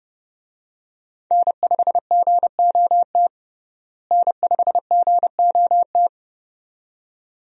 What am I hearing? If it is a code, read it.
Morse code, heard as a keyed tone, about 30 wpm, 700 Hz. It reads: N5GOT N5GOT
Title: USA Call Signs - 5 Characters - Repeated 2x - 30wpm